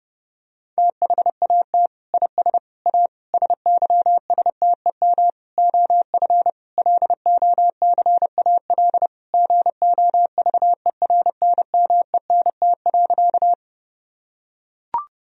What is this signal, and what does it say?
Morse code, heard as a keyed tone, about 30 wpm, 700 Hz. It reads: THAT IS A SYSTEM OF LOCAL GOVERNMENT.
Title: Sentences from Top 1000 Words - Review Part 3 - 30wpm